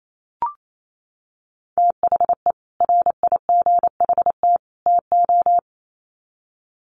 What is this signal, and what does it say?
Morse code, heard as a keyed tone, about 28 wpm, 700 Hz. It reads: THE RIGHT TO
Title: Sets of 3 Words 28wpm